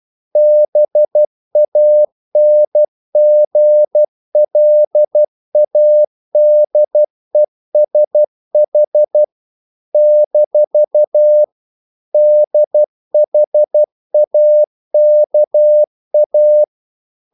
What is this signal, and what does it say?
Morse code, heard as a keyed tone, about 12 wpm, 600 Hz. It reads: BANGLADESH - DHAKA
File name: CW / Practice / NationalCapitals12